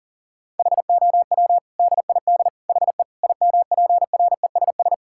Morse code, heard as a keyed tone, about 40 wpm, 700 Hz. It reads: HOW DID HE IMPRESS
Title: Sentences from Top 1600 Words 40wpm